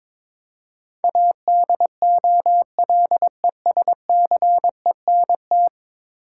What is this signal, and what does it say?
Morse code, heard as a keyed tone, about 22 wpm, 700 Hz. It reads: ADOLESCENT
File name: practice / 22wpm / 10 Letter Words - First 3 Letters Given - 22wpm